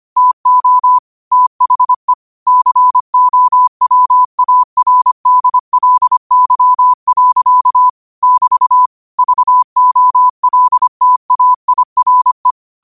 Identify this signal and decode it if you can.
Morse code, heard as a keyed tone, about 25 wpm, 1000 Hz. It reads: TO THE COWARDLY. <BT> VOLTAIRE
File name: qotdfiles / 25wpm